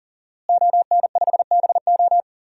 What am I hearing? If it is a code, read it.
Morse code, heard as a keyed tone, about 40 wpm, 700 Hz. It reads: ON5BO
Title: DXCC-100 Call Signs - 5 Characters - Repeated 2x - 40wpm